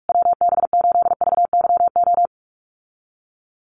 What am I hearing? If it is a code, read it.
Morse code, heard as a keyed tone, about 44 wpm, 690 Hz. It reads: WB8VYO
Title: USA Call Signs Repeated 3x 45wpm